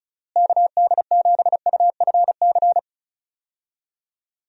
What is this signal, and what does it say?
Morse code, heard as a keyed tone, about 36 wpm, 700 Hz. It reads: KD7UFC
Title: USA Call Signs - 6 Characters - Repeated 3x - 35wpm